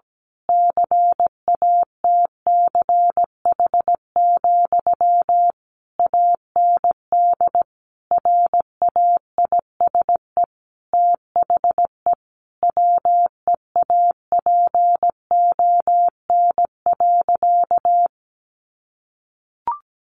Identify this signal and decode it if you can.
Morse code, heard as a keyed tone, about 17 wpm, 700 Hz. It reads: CATCH, AND RAISE THE WEAPON.
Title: A Sound Of Thunder 17wpm